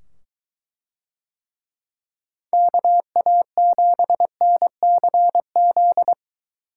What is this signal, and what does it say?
Morse code, heard as a keyed tone, about 22 wpm, 700 Hz. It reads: KA7NCZ